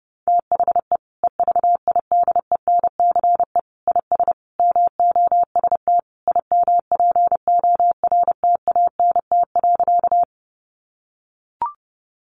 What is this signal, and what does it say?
Morse code, heard as a keyed tone, about 30 wpm, 700 Hz. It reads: THE EVIDENCE IS MOST IMPORTANT.
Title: Sentences from Top 1300 Words 30wpm